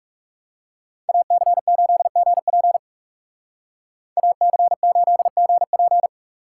Morse code, heard as a keyed tone, about 36 wpm, 700 Hz. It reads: AC8GP AC8GP
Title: Speed Racing - USA 2x2 Call Signs 30wpm